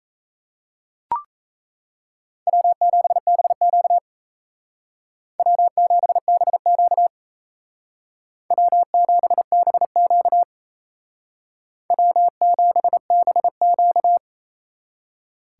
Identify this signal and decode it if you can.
Morse code, heard as a keyed tone, about 34 wpm, 700 Hz. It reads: W7BQ W7BQ W7BQ W7BQ
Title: Speed Racing - USA 1x2 Call Signs 28wpm